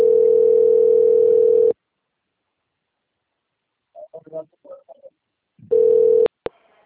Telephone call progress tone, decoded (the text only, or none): ringback tone